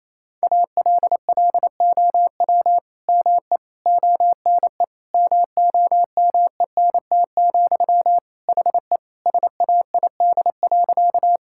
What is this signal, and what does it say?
Morse code, heard as a keyed tone, about 28 wpm, 700 Hz. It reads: ALLOW ME ONE MOMENT, HE SAID.
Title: Sentences from Top 1000 Words - Review Part 5 - 28wpm